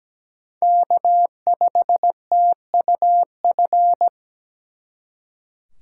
Morse code, heard as a keyed tone, about 20 wpm, 700 Hz. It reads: K5TUF
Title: Speed Racing - USA 1x3 Call Signs 17wpm